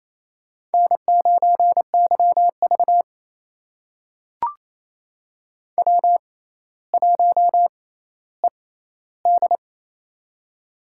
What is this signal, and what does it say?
Morse code, heard as a keyed tone, about 28 wpm, spaced as for 14 wpm, 700 Hz. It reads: N9YV W1ED
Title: USA Call Signs - 4 Characters - 6x Character Spacing - 28wpm